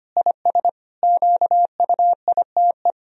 Morse code, heard as a keyed tone, about 25 wpm, 700 Hz. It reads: IS QUITE
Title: Sentences from Top 1000 Words - Review Part 4 - 25wpm